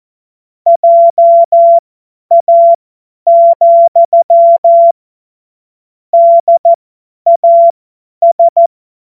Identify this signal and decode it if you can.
Morse code, heard as a keyed tone, about 14 wpm, spaced as for 10 wpm, 680 Hz. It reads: JA, DAS
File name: Morsekurs / Witze / Witz 15. November 2022 50 F70